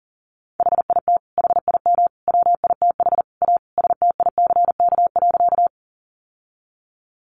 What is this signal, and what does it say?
Morse code, heard as a keyed tone, about 40 wpm, 700 Hz. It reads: HIT HIM WITH A STICK.
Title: Sentences from Top 1200 Words 40wpm